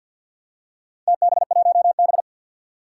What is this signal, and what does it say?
Morse code, heard as a keyed tone, about 51 wpm, 700 Hz. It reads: TB1B
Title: Four Challenging Letter-Number Combinations 50wpm